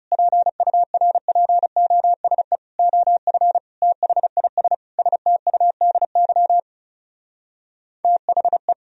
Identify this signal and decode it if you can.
Morse code, heard as a keyed tone, about 37 wpm, 700 Hz. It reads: PURPOSE OF THIS STUDY THE
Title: Speed-Racing - Sets of 5 Words 30wpm